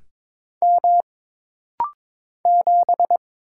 Morse code, heard as a keyed tone, about 22 wpm, 700 Hz. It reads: M 7
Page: This practice set is composed of Letters, Numbers, common punctuations, and common prosigns.